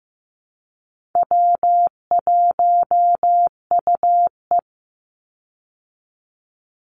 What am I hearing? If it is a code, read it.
Morse code, heard as a keyed tone, about 15 wpm, 700 Hz. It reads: W1UE
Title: Easy Call Signs Repeated 3x - 15wpm